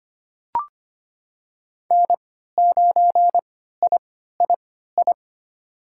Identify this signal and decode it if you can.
Morse code, heard as a keyed tone, about 25 wpm, spaced as for 14 wpm, 700 Hz. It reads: N9III
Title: USA Call Signs - 3x Character Spacing - 25wpm